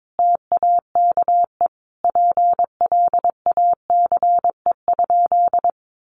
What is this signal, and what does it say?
Morse code, heard as a keyed tone, about 22 wpm, 700 Hz. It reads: TAKE PLACE?
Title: Sentences from Top 200 Words 22wpm